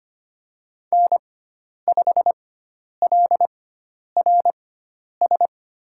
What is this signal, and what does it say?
Morse code, heard as a keyed tone, about 25 wpm, spaced as for 10 wpm, 700 Hz. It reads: N5LRS
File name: Lesson 16g - Letter c - Call Signs 25wpm